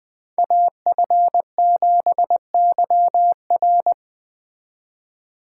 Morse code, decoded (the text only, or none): AF7YR